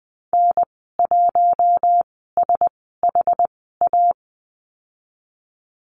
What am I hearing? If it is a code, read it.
Morse code, heard as a keyed tone, about 20 wpm, spaced as for 15 wpm, 700 Hz. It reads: N1SHA
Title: 088 Lesson 15g - Number 5 - Call Signs 20wpm